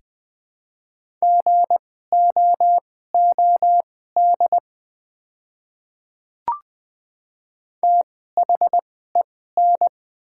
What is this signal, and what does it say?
Morse code, heard as a keyed tone, about 20 wpm, spaced as for 14 wpm, 700 Hz. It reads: GOOD THEN
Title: 2x Character Spacing - Top 300 Words - 20wpm